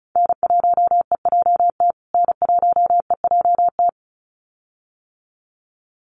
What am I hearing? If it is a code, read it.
Morse code, heard as a keyed tone, about 34 wpm, 700 Hz. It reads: N1EJT N1EJT
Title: USA Call Signs Repeated 2x 35wpm